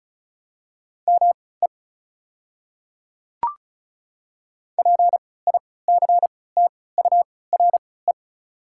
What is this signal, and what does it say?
Morse code, heard as a keyed tone, about 36 wpm, spaced as for 20 wpm, 690 Hz. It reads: ME PICTURE
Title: 3x Character Spacing - Top 200 Words - 35wpm